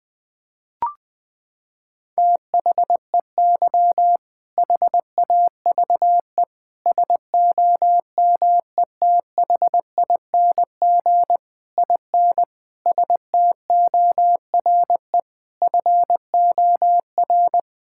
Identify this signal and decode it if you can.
Morse code, heard as a keyed tone, about 21 wpm, 700 Hz. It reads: THEY HAVE SOMETHING IN STORE FOR
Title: Sentences from Top 1000 Words - Review Part 5 - 20wpm